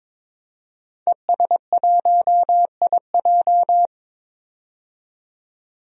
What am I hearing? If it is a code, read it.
Morse code, heard as a keyed tone, about 22 wpm, 700 Hz. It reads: ES1IJ
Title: DXCC-100 Call Signs - 5 Characters - 22wpm